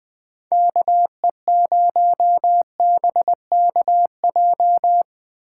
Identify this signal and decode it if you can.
Morse code, heard as a keyed tone, about 20 wpm, 700 Hz. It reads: KE0BKJ
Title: Speed Racing - USA 2x3 Call Signs 17wpm